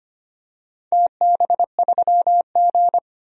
Morse code, heard as a keyed tone, about 25 wpm, 700 Hz. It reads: TB3G
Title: DXCC-100 Call Signs - 4 Characters - 25wpm